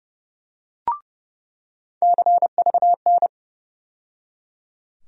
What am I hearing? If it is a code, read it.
Morse code, heard as a keyed tone, about 30 wpm, 700 Hz. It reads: CVN
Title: USA Call Sign Postfixes - 3 Letters 30wpm